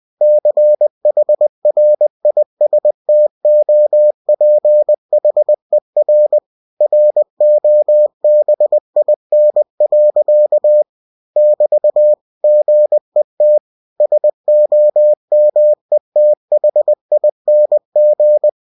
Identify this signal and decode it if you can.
Morse code, heard as a keyed tone, about 20 wpm, 600 Hz. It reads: CHRISTOPHER ROBIN. <BT> GET SOMETHING
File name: winnie-20wpm